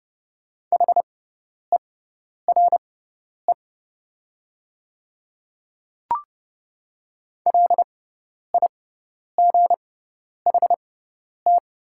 Morse code, heard as a keyed tone, about 30 wpm, spaced as for 10 wpm, 700 Hz. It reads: HERE LIGHT
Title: 6x Character Spacing - Top 300 Words - 30wpm